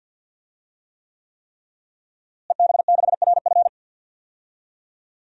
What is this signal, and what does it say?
Morse code, heard as a keyed tone, about 53 wpm, 700 Hz. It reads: EB6RF
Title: DXCC-100 Call Signs - 5 Characters - Repeated 3x - 50wpm